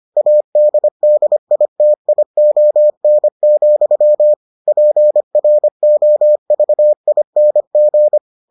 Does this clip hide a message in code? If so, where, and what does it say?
Yes: Morse code from 0.2 to 8.2 s, heard as a keyed tone, about 25 wpm, 600 Hz. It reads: ADDITION, PROVING